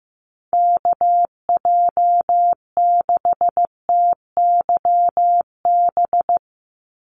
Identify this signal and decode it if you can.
Morse code, heard as a keyed tone, about 15 wpm, 700 Hz. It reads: KJ6TYB
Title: USA Call Signs 15wpm